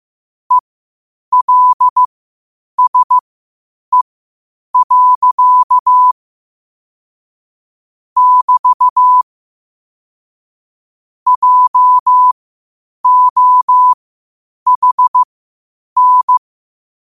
Morse code, heard as a keyed tone, about 15 wpm, spaced as for 8 wpm, 1000 Hz. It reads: ELSE. <BT> JOHN
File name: qotdfiles / 8wpm